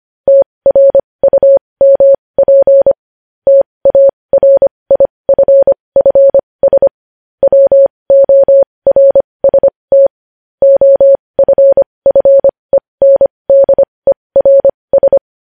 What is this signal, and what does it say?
Morse code, heard as a keyed tone, about 25 wpm, spaced as for 20 wpm, 550 Hz. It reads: TRUMP TARIFFS WORST OFFENDERS
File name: podcast / News headlines-1x-25x20